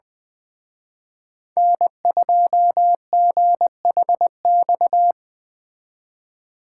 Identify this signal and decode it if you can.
Morse code, heard as a keyed tone, about 24 wpm, 700 Hz. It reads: N2GHX
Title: Speed Racing - USA Call Signs 20wpm